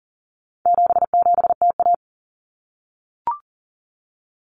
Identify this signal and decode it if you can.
Morse code, heard as a keyed tone, about 40 wpm, 700 Hz. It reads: 77TA
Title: Common Cut Numbers - 4-Digit 40wpm